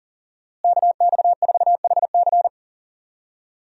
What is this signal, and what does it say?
Morse code, heard as a keyed tone, about 37 wpm, 700 Hz. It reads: KX4HC
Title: USA Call Signs - 5 Characters - Repeated 2x - 40wpm